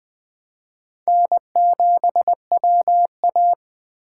Morse code, heard as a keyed tone, about 20 wpm, 700 Hz. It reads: N7WA
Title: Speed Racing - USA 1x2 Call Signs 17wpm